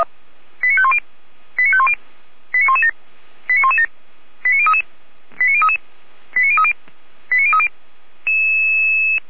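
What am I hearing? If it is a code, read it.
Five-tone selective call (ZVEI), heard as quick runs of five tones, tones of about 70 ms. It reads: 89510 89510 89117 89117 89033 89033 89030 89030
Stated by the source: Five-tone ZVEI selective call received by SDR.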